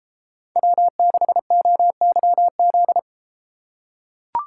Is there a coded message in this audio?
Yes: Morse code from 0.6 to 3.0 s, heard as a keyed tone, about 31 wpm, 700 Hz. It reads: W6OYZ